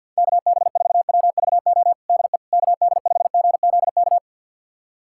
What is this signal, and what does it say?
Morse code, heard as a keyed tone, about 46 wpm, 690 Hz. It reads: KB4WVO DE KD5GZK
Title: USA Callsign DE Callsign 50wpm